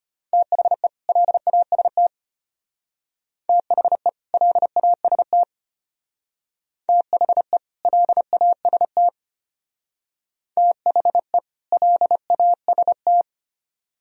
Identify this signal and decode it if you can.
Morse code, heard as a keyed tone, about 33 wpm, 700 Hz. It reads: THE LAST THE LAST THE LAST THE LAST